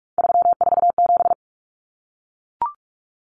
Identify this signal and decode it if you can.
Morse code, heard as a keyed tone, about 44 wpm, 700 Hz. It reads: 347
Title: Numbers - 3-Digit 45wpm